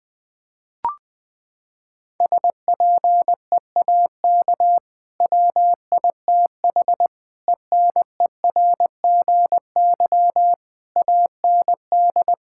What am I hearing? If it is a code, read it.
Morse code, heard as a keyed tone, about 20 wpm, 700 Hz. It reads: SPEAK WITH ENERGY AND
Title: Sentences from Top 1000 Words - Review Part 6 - 20wpm